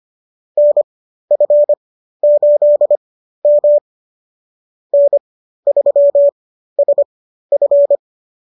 Morse code, heard as a keyed tone, about 25 wpm, spaced as for 13 wpm, 590 Hz. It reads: NF8M N3SF